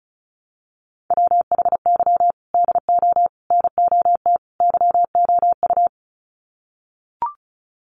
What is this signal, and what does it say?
Morse code, heard as a keyed tone, about 34 wpm, 700 Hz. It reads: WHY DO NOT YOU
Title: Sets of 4 Words 35wpm